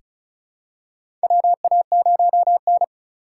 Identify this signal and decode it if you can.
Morse code, heard as a keyed tone, about 36 wpm, 690 Hz. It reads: WA0N
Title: USA Call Signs - 4 Characters - 3x Character Spacing - 35wpm